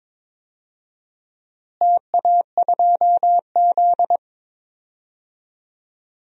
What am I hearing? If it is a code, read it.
Morse code, heard as a keyed tone, about 22 wpm, 700 Hz. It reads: TA2Z
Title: DXCC-100 Call Signs - 4 Characters - Repeated 3x - 22wpm